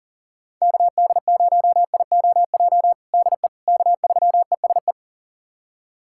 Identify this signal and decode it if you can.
Morse code, heard as a keyed tone, about 40 wpm, 700 Hz. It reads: KD0IOJ DE K3ESE
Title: USA Callsign DE Callsign 40wpm